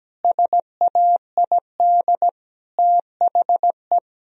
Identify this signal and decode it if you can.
Morse code, heard as a keyed tone, about 17 wpm, 700 Hz. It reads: SAID THE
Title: Sentences from Top 1000 Words - Review Part 3 - 17wpm